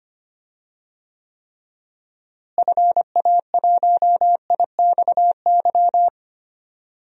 Morse code, heard as a keyed tone, about 25 wpm, 700 Hz. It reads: FA1IXY
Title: DXCC-100 Call Signs - 6 Characters - Repeated 2x - 25wpm